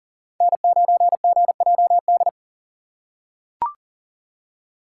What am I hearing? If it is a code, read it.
Morse code, heard as a keyed tone, about 40 wpm, 700 Hz. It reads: N9GJD
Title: USA Call Signs - 5 Characters - 4x Character Spacing - 40wpm